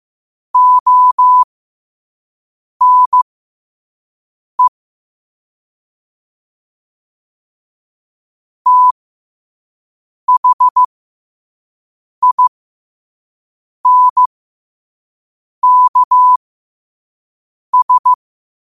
Morse code, heard as a keyed tone, about 15 wpm, spaced as for 5 wpm, 1000 Hz. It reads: ONE THINKS
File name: qotdfiles / 5wpm